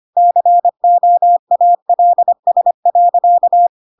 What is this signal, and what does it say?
Morse code, heard as a keyed tone, about 25 wpm, 700 Hz. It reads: COALS.
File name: Pinocchio / Pinocchio ch5 25 25 ik1hge